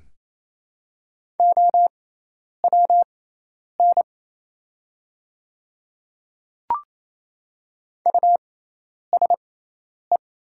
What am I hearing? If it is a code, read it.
Morse code, heard as a keyed tone, about 28 wpm, spaced as for 10 wpm, 700 Hz. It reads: OWN USE
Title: 6x Character Spacing - Top 300 Words - 28wpm